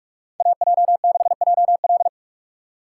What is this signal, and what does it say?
Morse code, heard as a keyed tone, about 58 wpm, 700 Hz. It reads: AJ6JL